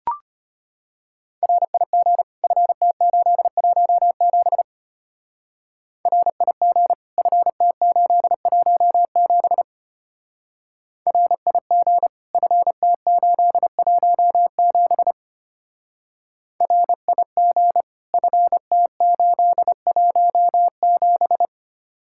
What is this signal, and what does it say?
Morse code, heard as a keyed tone, about 31 wpm, 700 Hz. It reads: RIG FT817 RIG FT817 RIG FT817 RIG FT817